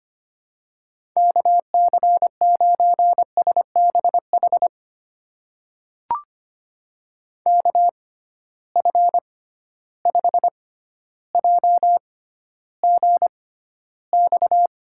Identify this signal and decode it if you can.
Morse code, heard as a keyed tone, about 25 wpm, spaced as for 13 wpm, 700 Hz. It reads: KC9SBH KF5JGX
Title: USA Call Signs - 6 Characters - 6x Character Spacing - 25wpm